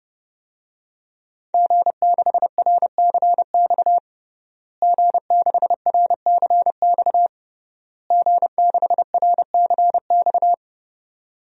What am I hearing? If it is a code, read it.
Morse code, heard as a keyed tone, about 30 wpm, 700 Hz. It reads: G6RCX G6RCX G6RCX
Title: DXCC-100 Call Signs - 5 Characters - Repeated 3x - 30wpm